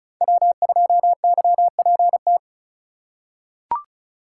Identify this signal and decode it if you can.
Morse code, heard as a keyed tone, about 34 wpm, 700 Hz. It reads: W2YPT
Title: USA Call Signs - 5 Characters - 4x Character Spacing - 35wpm